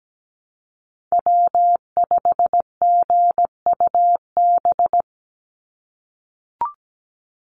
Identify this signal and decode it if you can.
Morse code, heard as a keyed tone, about 17 wpm, 700 Hz. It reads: W5GUB
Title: USA Call Signs 17wpm